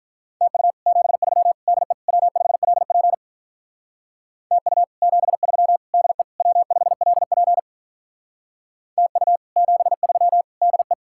Morse code, heard as a keyed tone, about 47 wpm, 700 Hz. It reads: TU 73 DE W5LP TU 73 DE W5LP TU 73 DE